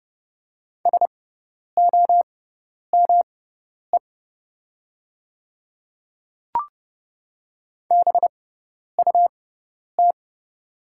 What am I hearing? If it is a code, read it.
Morse code, heard as a keyed tone, about 30 wpm, spaced as for 10 wpm, 700 Hz. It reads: SOME BUT
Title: 6x Character Spacing - Top 100 Words - 30wpm